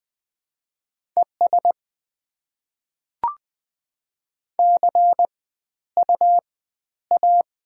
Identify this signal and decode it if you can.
Morse code, heard as a keyed tone, about 20 wpm, 700 Hz. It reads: ES C U A
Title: Top 20 QSO Elements - 4x Character Spacing - 20wpm